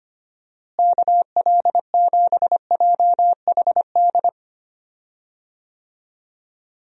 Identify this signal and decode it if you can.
Morse code, heard as a keyed tone, about 25 wpm, 700 Hz. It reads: KL7JHD
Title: USA Call Signs - 6 Characters - Repeated 3x - 25wpm